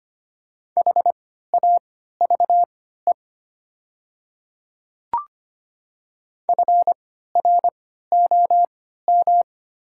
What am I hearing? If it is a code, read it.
Morse code, heard as a keyed tone, about 25 wpm, spaced as for 14 wpm, 700 Hz. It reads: HAVE FROM
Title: 3x Character Spacing - Top 100 Words - 25wpm